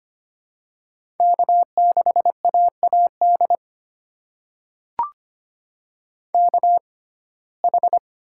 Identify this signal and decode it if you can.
Morse code, heard as a keyed tone, about 25 wpm, spaced as for 18 wpm, 700 Hz. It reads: K6AAD KH
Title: USA Call Signs - 5 Characters - 6x Character Spacing - 25wpm